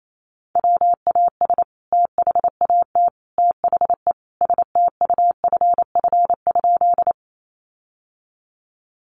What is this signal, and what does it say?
Morse code, heard as a keyed tone, about 28 wpm, 700 Hz. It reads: WAS THAT THE STUFF?
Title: Sentences from Top 1400 Words 28wpm